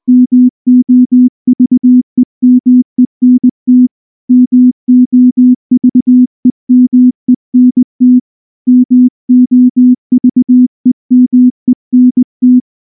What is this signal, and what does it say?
Morse code, heard as a keyed tone, about 19 wpm, 250 Hz. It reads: MOVEMENT MOVEMENT MOVEMENT
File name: morse / morse-training-20WPM-7